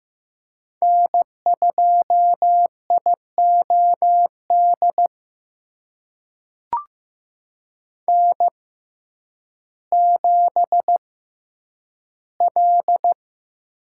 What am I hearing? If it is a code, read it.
Morse code, heard as a keyed tone, about 15 wpm, spaced as for 9 wpm, 700 Hz. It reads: N2IOD N7L
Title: USA Call Signs - 5 Characters - 6x Character Spacing - 15wpm